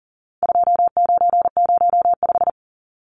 Morse code, heard as a keyed tone, about 40 wpm, 700 Hz. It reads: 2905
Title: Numbers - 4-Digit 40wpm